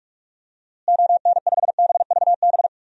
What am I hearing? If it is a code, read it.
Morse code, heard as a keyed tone, about 47 wpm, 700 Hz. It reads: ON5BVB
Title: DXCC-100 Call Signs - 6 Characters - Repeated 3x - 45wpm